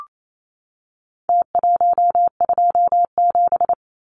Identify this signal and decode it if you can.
Morse code, heard as a keyed tone, about 28 wpm, 700 Hz. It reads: T127
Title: Numbers - Contest Style - 4-Digit 28wpm